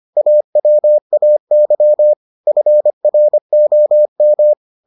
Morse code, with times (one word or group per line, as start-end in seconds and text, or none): 0.2-2.1 AWAY
2.5-4.5 FROM